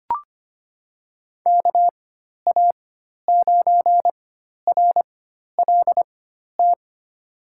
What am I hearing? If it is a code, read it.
Morse code, heard as a keyed tone, about 25 wpm, spaced as for 12 wpm, 700 Hz. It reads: KA9RLT
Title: USA Call Signs - 6 Characters - 4x Character Spacing - 25wpm